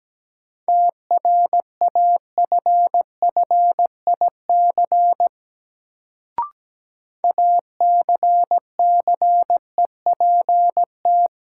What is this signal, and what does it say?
Morse code, heard as a keyed tone, about 17 wpm, 700 Hz. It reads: TRAFFIC ACCEPT
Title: Sentences from Top 1300 Words - New Word Drill - 17wpm